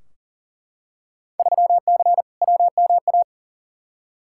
Morse code, heard as a keyed tone, about 40 wpm, 700 Hz. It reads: 3C WMA
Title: Field Day Exchanges Encore - 40wpm